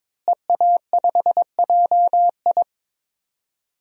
Morse code, heard as a keyed tone, about 22 wpm, 700 Hz. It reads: EA5JI
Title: DXCC-100 Call Signs - 5 Characters - Repeated 3x - 22wpm